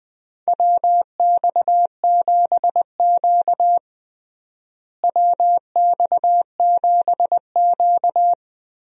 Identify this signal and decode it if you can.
Morse code, heard as a keyed tone, about 20 wpm, 700 Hz. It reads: WX7Q WX7Q
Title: USA Call Signs - 4 Characters - Repeated 3x - 20wpm